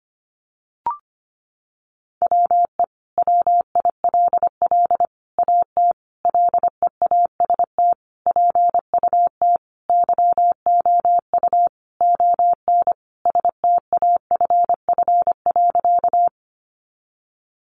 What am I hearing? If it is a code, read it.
Morse code, heard as a keyed tone, about 25 wpm, 700 Hz. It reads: WE WILL AT LEAST PUT YOU ON STAFF.
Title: Sentences from Top 1200 Words 25wpm